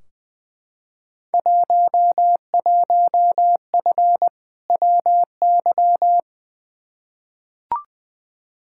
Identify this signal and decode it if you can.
Morse code, heard as a keyed tone, about 20 wpm, 700 Hz. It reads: 11F WY
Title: Field Day Exchanges Encore - 20wpm